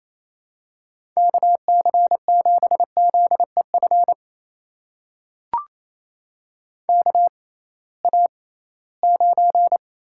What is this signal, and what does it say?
Morse code, heard as a keyed tone, about 28 wpm, spaced as for 18 wpm, 700 Hz. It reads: KC7ZEF KA9
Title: USA Call Signs - 6 Characters - 6x Character Spacing - 28wpm